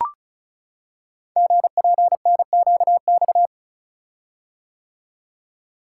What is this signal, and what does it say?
Morse code, heard as a keyed tone, about 33 wpm, 700 Hz. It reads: GPNQX
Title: Code Groups - 5 Letter:Numbers - 35wpm